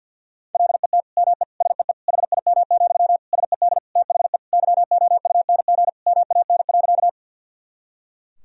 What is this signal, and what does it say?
Morse code, heard as a keyed tone, about 50 wpm, 700 Hz. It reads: LET ME SEE HIM, SED THE YOUNG MAN.
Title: Sentences from Top 400 Words and Abbreviations 50wpm